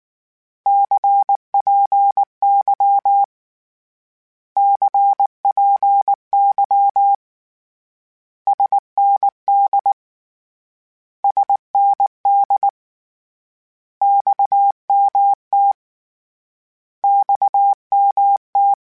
Morse code, heard as a keyed tone, about 19 wpm, 800 Hz. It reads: CPY CPY SND SND XMT XMT
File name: Abreviations / Abreviations mp3s / Abreviations@18